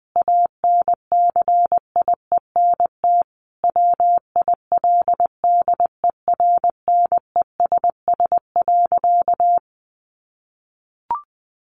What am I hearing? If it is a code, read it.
Morse code, heard as a keyed tone, about 20 wpm, 700 Hz. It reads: ANCIENT WILDERNESS.
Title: A Sound Of Thunder 20wpm